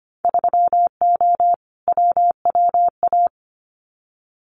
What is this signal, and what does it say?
Morse code, heard as a keyed tone, about 25 wpm, 700 Hz. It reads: 3O WWA
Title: Winter Field Day Exchanges Encore 25wpm